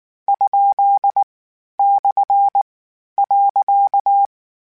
Morse code, heard as a keyed tone, about 19 wpm, 800 Hz. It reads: ? / .
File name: Misc / Misc-mp3s / Punct @18